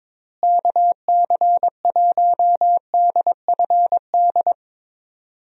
Morse code, heard as a keyed tone, about 22 wpm, 700 Hz. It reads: KC1DFD
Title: USA Call Signs - 6 Characters - Repeated 2x - 22wpm